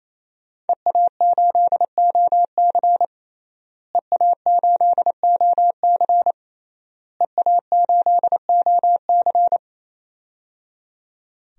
Morse code, heard as a keyed tone, about 28 wpm, 700 Hz. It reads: EA8OC EA8OC EA8OC
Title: DXCC-100 Call Signs - 5 Characters - Repeated 3x - 28wpm